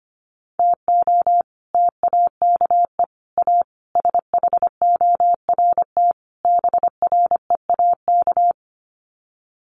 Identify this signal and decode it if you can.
Morse code, heard as a keyed tone, about 25 wpm, 700 Hz. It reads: TO TAKE A SHORT BREAK
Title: Sets of 5 Words Encore2 25wpm